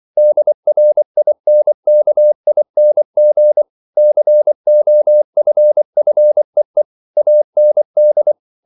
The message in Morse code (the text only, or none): DRINKING COFFEE AND